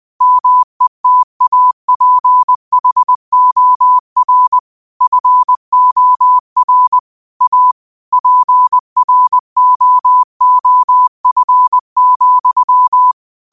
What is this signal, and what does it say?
Morse code, heard as a keyed tone, about 20 wpm, 1000 Hz. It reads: METAPHOR FOR A PROOF,
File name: qotdfiles / 20wpm